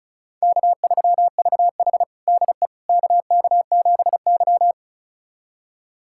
Morse code, heard as a keyed tone, about 36 wpm, 700 Hz. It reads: K3VH DE KK7Y
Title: USA Callsign DE Callsign - 4 Characters - 35wpm